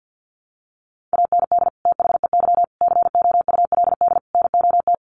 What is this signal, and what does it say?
Morse code, heard as a keyed tone, about 50 wpm, 700 Hz. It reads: AND THEY COULD NOT
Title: Sentences from Top 1100 Words 50wpm